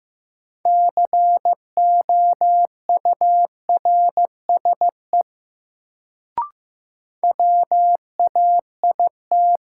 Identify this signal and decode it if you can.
Morse code, heard as a keyed tone, about 15 wpm, 700 Hz. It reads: COURSE WAIT
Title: Sentences from Top 500 Words - New Word Drill - 15wpm